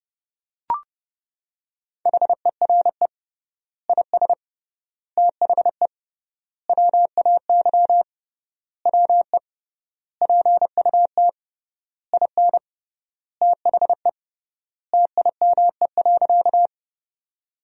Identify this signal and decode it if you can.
Morse code, heard as a keyed tone, about 30 wpm, 700 Hz. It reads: HERE IS THE WAY WE PUT IN THE TIME.
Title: 3x Word-Spacing - Sentences from Top 200 Words - 30wpm